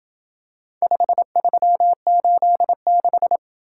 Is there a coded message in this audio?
Yes: Morse code from 0.8 to 3.4 s, heard as a keyed tone, about 27 wpm, 700 Hz. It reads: 5386